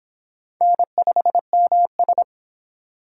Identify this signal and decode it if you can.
Morse code, heard as a keyed tone, about 30 wpm, 700 Hz. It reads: N5MS